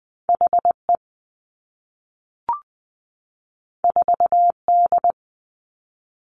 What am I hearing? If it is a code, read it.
Morse code, heard as a keyed tone, about 20 wpm, 700 Hz. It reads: H E 4 D
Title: Two Challenging Letter-Number Combinations 20wpm